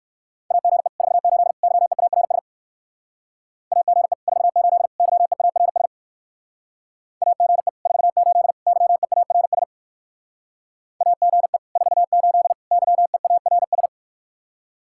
Morse code, heard as a keyed tone, about 59 wpm, 700 Hz. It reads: AGE 48 YEARS AGE 48 YEARS AGE 48 YEARS AGE 48 YEARS